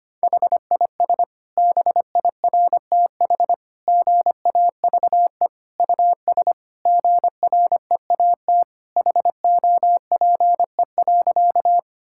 Morse code, heard as a keyed tone, about 25 wpm, 700 Hz. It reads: HIS BIRTH GAVE US GREAT HOPE.
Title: Sentences from Top 1000 Words - Review Part 5 - 25wpm